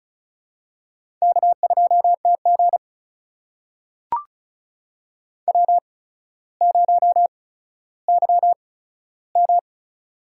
Morse code, heard as a keyed tone, about 36 wpm, spaced as for 15 wpm, 700 Hz. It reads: K2TG W0YM
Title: USA Call Signs - 4 Characters - 8x Character Spacing - 35wpm